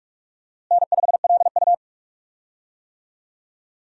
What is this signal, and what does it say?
Morse code, heard as a keyed tone, about 42 wpm, 700 Hz. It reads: N5LU
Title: USA Call Signs - 4 Characters - Repeated 3x - 45wpm